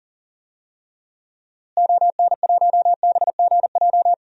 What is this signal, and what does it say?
Morse code, heard as a keyed tone, about 38 wpm, 700 Hz. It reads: ON1BGJ
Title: DXCC-100 Call Signs - 6 Characters - Repeated 2x - 40wpm